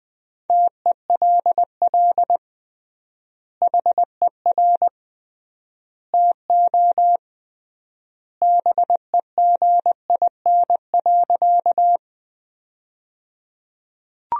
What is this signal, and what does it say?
Morse code, heard as a keyed tone, about 21 wpm, 700 Hz. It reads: TELL HER TO BEGIN.
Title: 3x Word-Spacing - Sentences from Top 300 Words - 20wpm